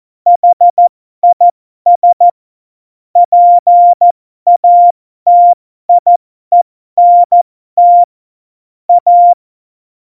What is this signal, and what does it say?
Morse code, heard as a keyed tone, about 14 wpm, spaced as for 12 wpm, 700 Hz. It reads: HIS PATIENT A
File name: Morsekurs / Witze